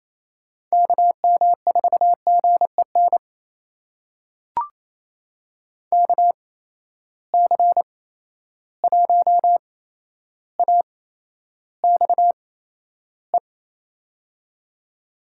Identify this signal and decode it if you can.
Morse code, heard as a keyed tone, about 28 wpm, spaced as for 12 wpm, 700 Hz. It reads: KM4GEN KC1AXE
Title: USA Call Signs - 6 Characters - 8x Character Spacing - 28wpm